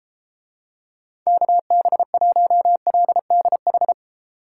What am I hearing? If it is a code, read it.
Morse code, heard as a keyed tone, about 32 wpm, 700 Hz. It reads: KB1LDH